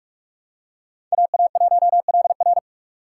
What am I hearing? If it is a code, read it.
Morse code, heard as a keyed tone, about 41 wpm, 700 Hz. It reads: AA1LR